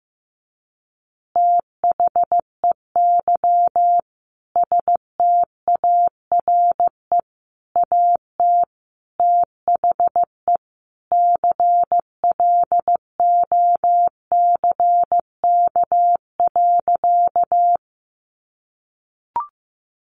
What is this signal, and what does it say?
Morse code, heard as a keyed tone, about 15 wpm, 700 Hz. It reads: THEY STARE AT THE CLOCK.
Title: Easy Phrases 15wpm